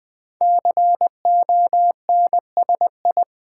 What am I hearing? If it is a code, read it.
Morse code, heard as a keyed tone, about 20 wpm, 700 Hz. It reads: CONSI
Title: Build A Word - 5 Letters At A Time - 12-Letter Words - 20wpm